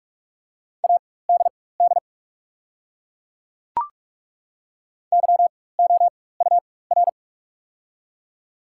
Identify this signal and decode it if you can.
Morse code, heard as a keyed tone, about 42 wpm, spaced as for 21 wpm, 690 Hz. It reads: ADD YOUR
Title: 4x Character Spacing - Top 200 Words - 45wpm